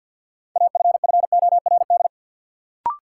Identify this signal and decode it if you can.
Morse code, heard as a keyed tone, about 50 wpm, 700 Hz. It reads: AFFORD
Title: Sentences from Top 1100 Words - New Word Drill - 50wpm